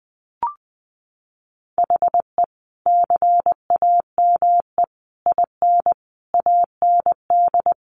Morse code, heard as a keyed tone, about 20 wpm, 700 Hz. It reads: HE CAME IN AND
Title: Sentences from Top 1400 Words 20wpm